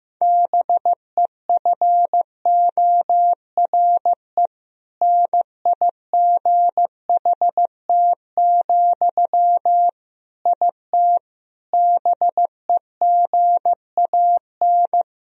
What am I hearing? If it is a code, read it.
Morse code, heard as a keyed tone, about 16 wpm, 700 Hz. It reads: BEFORE NIGHT, IT BEGAN
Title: Sentences from Top 1000 Words - Review Part 1 - 15wpm